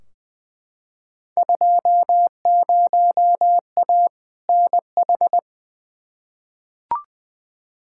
Morse code, heard as a keyed tone, about 21 wpm, 700 Hz. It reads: 20A NH
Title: Field Day Exchanges Encore - 20wpm